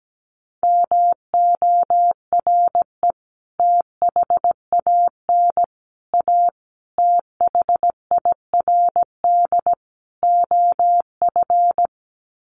Morse code, heard as a keyed tone, about 17 wpm, 700 Hz. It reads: MORE THAN A THIRD OF
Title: Sets of 5 Words Encore2 17wpm